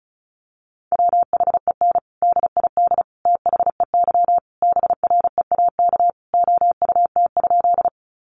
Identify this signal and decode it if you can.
Morse code, heard as a keyed tone, about 34 wpm, 700 Hz. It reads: WHEN DID THEY BREAK OUT?
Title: Sentences from Top 900 Words 35wpm